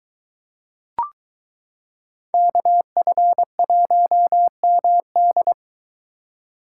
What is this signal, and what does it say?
Morse code, heard as a keyed tone, about 23 wpm, 700 Hz. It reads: KF1MD